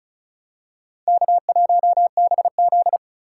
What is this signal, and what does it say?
Morse code, heard as a keyed tone, about 34 wpm, 700 Hz. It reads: K1BZ
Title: K1USN SST QSO Practice - 35wpm